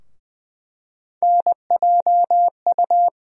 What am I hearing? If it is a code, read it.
Morse code, heard as a keyed tone, about 20 wpm, 700 Hz. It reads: NJU
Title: USA Call Sign Postfixes - 3 Letters 20wpm